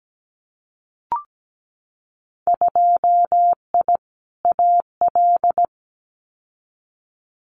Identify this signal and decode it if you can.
Morse code, heard as a keyed tone, about 17 wpm, 700 Hz. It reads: 2I AL
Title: Winter Field Day Exchanges 17wpm